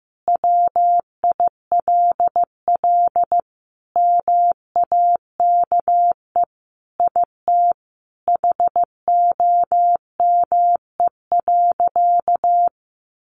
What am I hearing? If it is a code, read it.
Morse code, heard as a keyed tone, about 15 wpm, 700 Hz. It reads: WILL MAKE IT HOME.
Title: Sentences from Top 1500 Words 15wpm